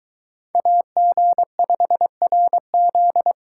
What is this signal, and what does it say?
Morse code, heard as a keyed tone, about 22 wpm, 700 Hz. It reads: AG5RZ